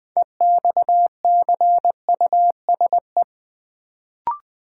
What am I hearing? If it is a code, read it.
Morse code, heard as a keyed tone, about 20 wpm, 700 Hz. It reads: EXCUSE
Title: Build A Word - 2 Letters At A Time - 6-Letter Words - 20wpm